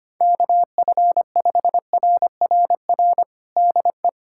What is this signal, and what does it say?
Morse code, heard as a keyed tone, about 25 wpm, 700 Hz. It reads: KF5RRR DE
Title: USA Callsign DE Callsign - 6 Characters - 25wpm